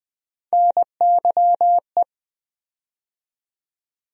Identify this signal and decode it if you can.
Morse code, heard as a keyed tone, about 20 wpm, 700 Hz. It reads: NYE
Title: USA Call Sign Postfixes - 3 Letters 20wpm